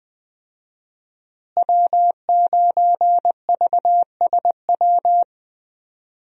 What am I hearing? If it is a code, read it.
Morse code, heard as a keyed tone, about 20 wpm, 700 Hz. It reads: W9VSW
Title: USA Call Signs - 5 Characters - Repeated 3x - 20wpm